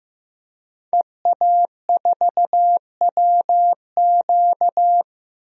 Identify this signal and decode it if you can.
Morse code, heard as a keyed tone, about 15 wpm, 700 Hz. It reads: EA4WQ
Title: DXCC-100 Call Signs - 5 Characters - 15wpm